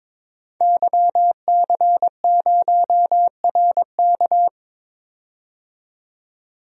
Morse code, heard as a keyed tone, about 21 wpm, 700 Hz. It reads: YC0RK
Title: DXCC-100 Call Signs - 5 Characters - 22wpm